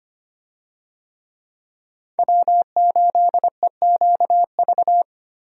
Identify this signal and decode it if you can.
Morse code, heard as a keyed tone, about 25 wpm, 700 Hz. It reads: W8EQV
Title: USA Call Signs - 5 Characters - Repeated 3x - 25wpm